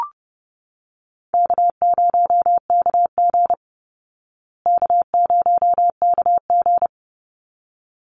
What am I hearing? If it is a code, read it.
Morse code, heard as a keyed tone, about 30 wpm, 700 Hz. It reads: K0KG K0KG
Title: Easy Call Signs Repeated 2x - 30wpm